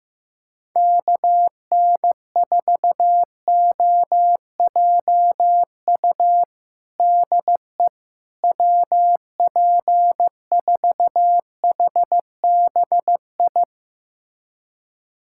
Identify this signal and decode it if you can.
Morse code, heard as a keyed tone, about 15 wpm, 700 Hz. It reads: KN4OJU DE WP4HBI
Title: USA Callsign DE Callsign 15wpm